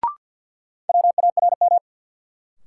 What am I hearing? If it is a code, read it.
Morse code, heard as a keyed tone, about 52 wpm, 700 Hz. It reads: WARM
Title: Sentences from Top 500 Words - New Word Drill - 50wpm